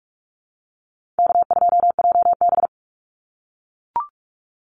Morse code, heard as a keyed tone, about 44 wpm, 700 Hz. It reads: K2JB
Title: NA SOTA Activator Callsigns 45wpm